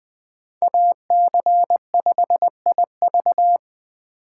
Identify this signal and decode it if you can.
Morse code, heard as a keyed tone, about 20 wpm, 700 Hz. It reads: AC5IV
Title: USA Call Signs - 5 Characters - 8x Character Spacing - 20wpm